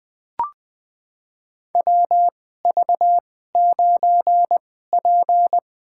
Morse code, heard as a keyed tone, about 20 wpm, spaced as for 15 wpm, 700 Hz. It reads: WV9P
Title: USA Call Signs - 4 Characters - 2x Character Spacing - 20wpm